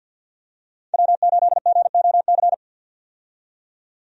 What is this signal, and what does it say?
Morse code, heard as a keyed tone, about 51 wpm, 700 Hz. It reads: W8GOC
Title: USA Call Signs - 5 Characters - 50wpm